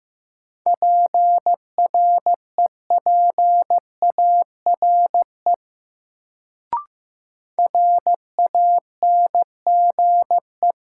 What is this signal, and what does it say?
Morse code, heard as a keyed tone, about 16 wpm, 700 Hz. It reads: PREPARE RANGE
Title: Sentences from Top 900 Words - New Word Drill - 15wpm